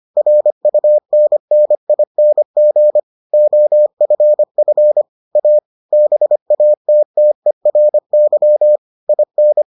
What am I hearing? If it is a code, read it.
Morse code, heard as a keyed tone, about 25 wpm, 600 Hz. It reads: RUNNING OFF A BATTERY IN